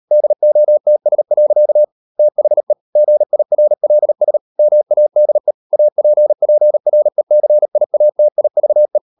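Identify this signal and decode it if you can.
Morse code, heard as a keyed tone, about 38 wpm, 600 Hz. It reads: DOTS. THE GIRLS MADE APPRECIATIVE
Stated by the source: Morse code practice at 38 WPM, sending Ken Follett's novel Never in English, from CW med Gnister.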